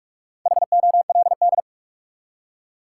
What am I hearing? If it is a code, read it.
Morse code, heard as a keyed tone, about 46 wpm, 700 Hz. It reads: HOLD
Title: Four Letter Words - 45wpm